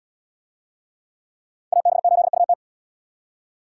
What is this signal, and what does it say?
Morse code, heard as a keyed tone, about 66 wpm, 700 Hz. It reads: AB8UT